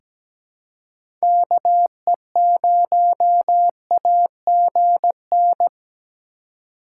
Morse code, heard as a keyed tone, about 17 wpm, 700 Hz. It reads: KE0AGN
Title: USA Call Signs - 6 Characters - 2x Character Spacing - 17wpm